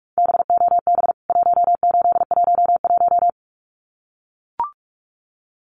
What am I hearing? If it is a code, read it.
Morse code, heard as a keyed tone, about 44 wpm, 700 Hz. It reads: BOB 1811
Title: CWT Exchange 45wpm